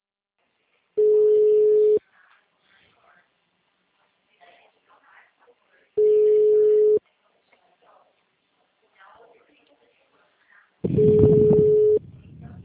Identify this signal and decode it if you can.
Ringback tone (Europe, 425 Hz), with speech.
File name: out- -2006- - - .64